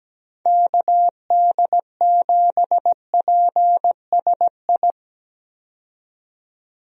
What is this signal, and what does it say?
Morse code, heard as a keyed tone, about 17 wpm, 700 Hz. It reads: KD7PSI
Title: USA Call Signs - 6 Characters - 17wpm